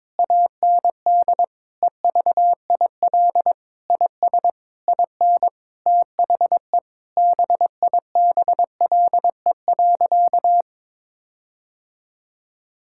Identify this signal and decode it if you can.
Morse code, heard as a keyed tone, about 22 wpm, 700 Hz. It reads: AND EVIL IS IN THE BIBLE.
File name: Sentences with 9-Word Phrases - 22wpm